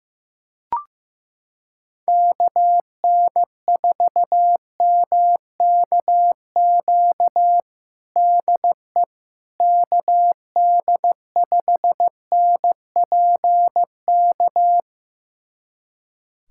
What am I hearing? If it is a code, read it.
Morse code, heard as a keyed tone, about 15 wpm, 700 Hz. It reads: KN4MKQ DE KD5NPK
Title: USA Callsign DE Callsign - 6 Characters - 15wpm